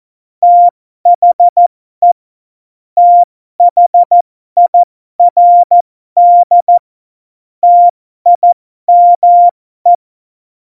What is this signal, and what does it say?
Morse code, heard as a keyed tone, about 14 wpm, spaced as for 12 wpm, 700 Hz. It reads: THE THIRD TIME
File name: Morsekurs / Witze